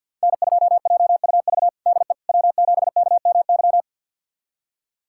Morse code, heard as a keyed tone, about 47 wpm, 700 Hz. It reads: N2JUV DE W7KMY
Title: USA Callsign DE Callsign - 5 Characters - 50wpm